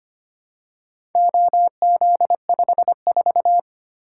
Morse code, heard as a keyed tone, about 25 wpm, 700 Hz. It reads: OZ54
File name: practice / 25wpm / Four Letter-Number Combinations - 25wpm